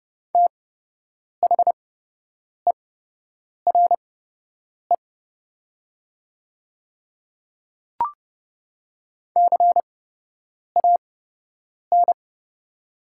Morse code, heard as a keyed tone, about 30 wpm, spaced as for 8 wpm, 700 Hz. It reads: THERE CAN
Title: 8x Character Spacing - Top 200 Words - 30wpm